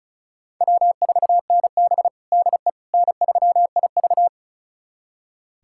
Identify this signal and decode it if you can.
Morse code, heard as a keyed tone, about 36 wpm, 700 Hz. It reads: W4NB DE N3IV
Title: USA Callsign DE Callsign - 4 Characters - 35wpm